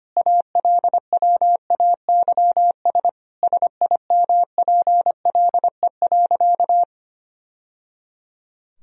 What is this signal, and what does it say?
Morse code, heard as a keyed tone, about 25 wpm, 700 Hz. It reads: ALWAYS SIMPLE.
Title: Sentences from Top 1000 Words - Review Part 3 - 25wpm